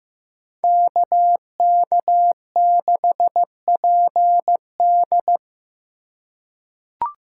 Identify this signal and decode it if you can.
Morse code, heard as a keyed tone, about 15 wpm, 700 Hz. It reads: KK6PD
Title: USA Call Signs - 5 Characters - 3x Character Spacing - 15wpm